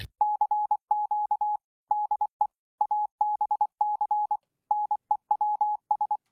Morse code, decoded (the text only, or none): CQ DE ABC NEWS